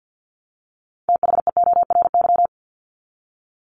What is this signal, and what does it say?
Morse code, heard as a keyed tone, about 50 wpm, 700 Hz. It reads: THEORY
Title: Top 501-750 Words 50wpm